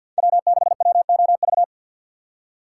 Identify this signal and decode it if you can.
Morse code, heard as a keyed tone, about 50 wpm, 700 Hz. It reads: W6WOV
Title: USA Call Signs - 5 Characters - 6x Character Spacing - 50wpm